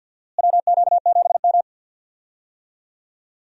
Morse code, heard as a keyed tone, about 47 wpm, 700 Hz. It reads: WQ7M
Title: USA Call Signs - 4 Characters - Repeated 3x - 50wpm